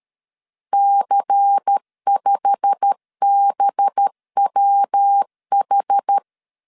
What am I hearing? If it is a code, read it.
Morse code, heard as a keyed tone, about 13 wpm, 800 Hz. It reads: C5BWH